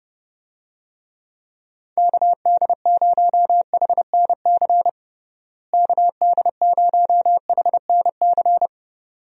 Morse code, heard as a keyed tone, about 30 wpm, 700 Hz. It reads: KD0HNC KD0HNC
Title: USA Call Signs - 6 Characters - Repeated 2x - 30wpm